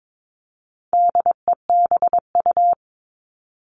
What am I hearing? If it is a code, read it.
Morse code, heard as a keyed tone, about 22 wpm, 700 Hz. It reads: DEBU
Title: All Cut Numbers - 4-Digit 22wpm